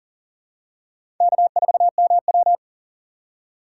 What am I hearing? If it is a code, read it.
Morse code, heard as a keyed tone, about 39 wpm, 700 Hz. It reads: K4MW
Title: USA Call Signs - 4 Characters - 3x Character Spacing - 40wpm